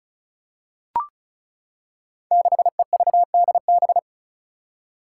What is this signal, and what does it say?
Morse code, heard as a keyed tone, about 34 wpm, 700 Hz. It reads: 6EVDB
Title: Five Challenging Letter-Number Combinations 35wpm